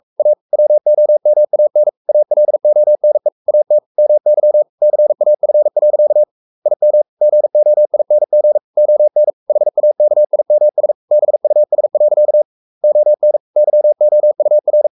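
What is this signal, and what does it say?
Morse code, heard as a keyed tone, about 44 wpm, 600 Hz. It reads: A WOMAN ALONE AT MY CAF. IM GOING ON HAKIMS BUS. ON YOUR
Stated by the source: Morse code at 43 wpm sending Ken Follett's novel Never in English, from CW med Gnister.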